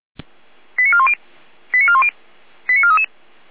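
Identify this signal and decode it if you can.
Five-tone selective call (ZVEI), heard as quick runs of five tones, tones of about 70 ms. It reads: 89410 89410 89433
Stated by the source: ZVEI five-tone call received by SDR.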